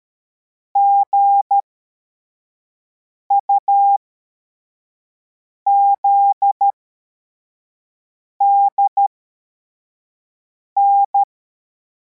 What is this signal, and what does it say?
Morse code, heard as a keyed tone, about 13 wpm, spaced as for 4 wpm, 800 Hz. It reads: GUZDN
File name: formation / cw / Text62